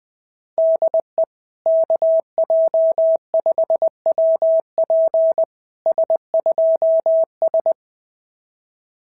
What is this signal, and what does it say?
Morse code, heard as a keyed tone, about 20 wpm, 660 Hz. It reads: DE KJ5WP S2S
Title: SOTA QSO Practice 20wpm